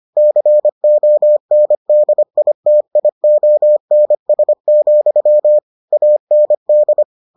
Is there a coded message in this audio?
Yes: Morse code from 0.2 to 7.0 s, heard as a keyed tone, about 25 wpm, 600 Hz. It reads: CONDITIONS, AND